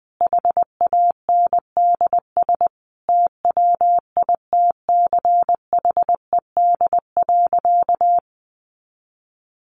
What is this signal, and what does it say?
Morse code, heard as a keyed tone, about 20 wpm, 700 Hz. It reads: HANDS TWITCHED.
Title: A Sound Of Thunder 20wpm